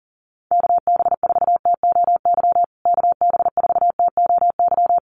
Morse code, heard as a keyed tone, about 40 wpm, 700 Hz. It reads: KB4TOY KB4TOY
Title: USA Call Signs Repeated 2x 40wpm